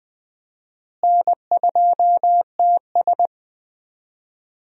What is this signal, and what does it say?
Morse code, heard as a keyed tone, about 20 wpm, 700 Hz. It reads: N2TS
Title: USA Call Signs - 4 Characters - Repeated 3x - 20wpm